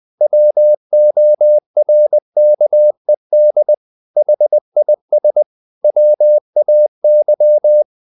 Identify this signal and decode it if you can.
Morse code, heard as a keyed tone, about 20 wpm, 600 Hz. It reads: WORKED HIS WAY
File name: the old man and the sea part 2 20wpm